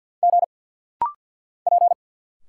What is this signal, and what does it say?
Morse code, heard as a keyed tone, about 46 wpm, 690 Hz. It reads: G P
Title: Single Letters - Rapid-Fire 50wpm